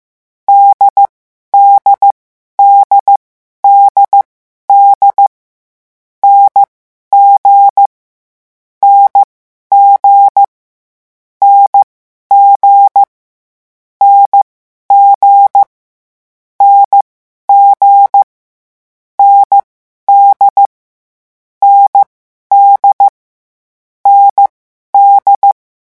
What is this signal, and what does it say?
Morse code, heard as a keyed tone, about 15 wpm, spaced as for 11 wpm, 800 Hz. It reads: DDDDD NG NG NG NG NG ND ND ND
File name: morse mp3 / lecon4-vitesse 14